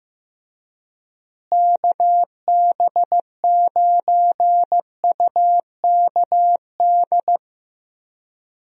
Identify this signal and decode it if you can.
Morse code, heard as a keyed tone, about 15 wpm, 700 Hz. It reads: KB9UKD
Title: USA Call Signs - 6 Characters - Repeated 3x - 15wpm